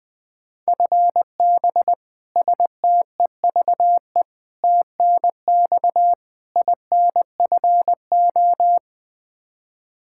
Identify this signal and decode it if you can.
Morse code, heard as a keyed tone, about 20 wpm, 700 Hz. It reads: FB STEVE TNX INFO
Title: Practice QSO - Exchange 3 - More Info - 20wpm